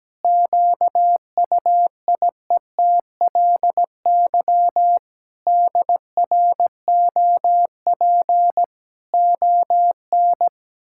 Morse code, heard as a keyed tone, about 17 wpm, 700 Hz. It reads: QUIETLY DROP ON
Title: Pangram Challenge v2 - 17wpm